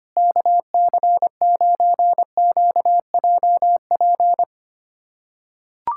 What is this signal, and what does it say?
Morse code, heard as a keyed tone, about 25 wpm, 700 Hz. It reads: KC9QJP
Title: USA Callsign DE Callsign - 6 Characters - 25wpm